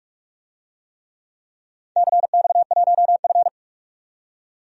Morse code, heard as a keyed tone, about 44 wpm, 700 Hz. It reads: CX1F
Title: DXCC-100 Call Signs - 4 Characters - 45wpm